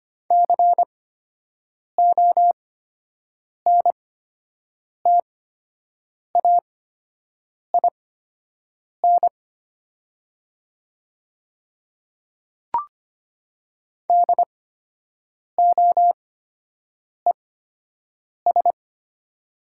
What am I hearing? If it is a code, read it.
Morse code, heard as a keyed tone, about 25 wpm, spaced as for 7 wpm, 700 Hz. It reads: CONTAIN DOES
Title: 8x Character Spacing - Top 500 Words - 25wpm